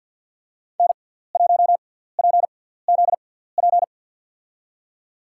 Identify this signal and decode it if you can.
Morse code, heard as a keyed tone, about 47 wpm, spaced as for 17 wpm, 700 Hz. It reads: N1PZP
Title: USA Call Signs - 5 Characters - 6x Character Spacing - 50wpm